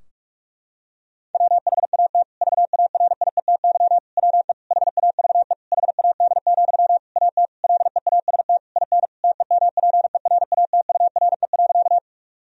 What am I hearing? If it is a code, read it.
Morse code, heard as a keyed tone, about 44 wpm, 700 Hz. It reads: WHAT VARIETY WE HAVE HAD, AT LEAST IN TEMPERATURE.
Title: Sentences from Top 1000 Words - Review Part 4 - 45wpm